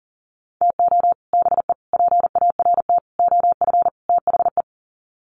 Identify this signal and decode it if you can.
Morse code, heard as a keyed tone, about 40 wpm, 700 Hz. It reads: TO BE PART OF THE
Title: Sets of 5 Words 40wpm